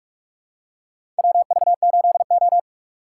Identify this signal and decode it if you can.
Morse code, heard as a keyed tone, about 46 wpm, 700 Hz. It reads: WV8O
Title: USA Call Signs - 4 Characters - 3x Character Spacing - 45wpm